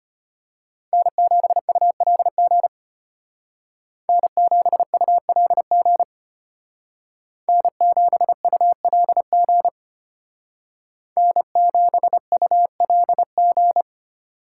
Morse code, heard as a keyed tone, about 32 wpm, 700 Hz. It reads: N7ULG N7ULG N7ULG N7ULG